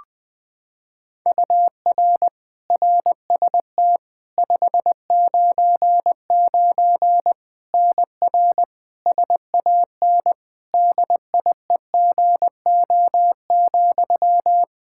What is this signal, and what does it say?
Morse code, heard as a keyed tone, about 20 wpm, 700 Hz. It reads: UR RST 599 NR SAN DIEGO,
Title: Practice QSO - Exchange 1 - Beginning - 20wpm